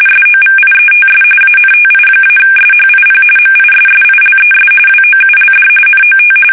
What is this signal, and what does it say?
RTTY, about 45.45 baud. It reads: SOVIETS ANTI-SHIP MISSILES WOULD NOT
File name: fsk850 / battleshipnj